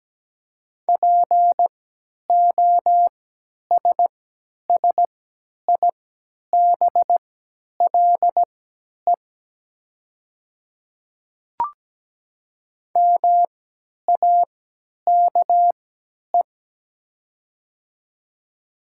Morse code, heard as a keyed tone, about 17 wpm, spaced as for 10 wpm, 700 Hz. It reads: POSSIBLE MAKE
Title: 3x Character Spacing - Top 500 Words - 17wpm